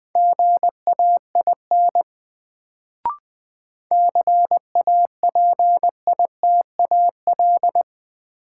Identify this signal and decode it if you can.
Morse code, heard as a keyed tone, about 20 wpm, 700 Hz. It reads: GAIN CAPITAL
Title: Sentences from Top 1000 Words - New Word Drill - 20wpm